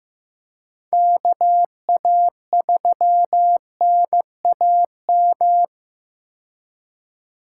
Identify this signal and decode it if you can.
Morse code, heard as a keyed tone, about 15 wpm, 700 Hz. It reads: KA3NAM
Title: K1USN SST QSO Practice - 15wpm